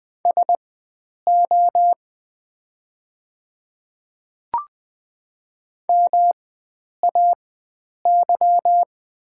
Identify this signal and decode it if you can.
Morse code, heard as a keyed tone, about 20 wpm, spaced as for 9 wpm, 690 Hz. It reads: SO MAY
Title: 4x Character Spacing - Top 100 Words - 20wpm